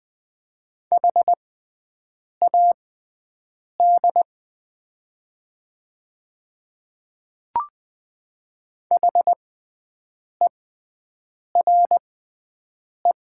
Morse code, heard as a keyed tone, about 20 wpm, spaced as for 7 wpm, 700 Hz. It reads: HAD HERE
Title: 6x Character Spacing - Top 200 Words - 20wpm